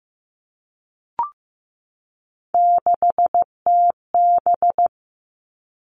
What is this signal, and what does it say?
Morse code, heard as a keyed tone, about 15 wpm, 700 Hz. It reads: 6TB
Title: All Cut Numbers - 3-Digit 15wpm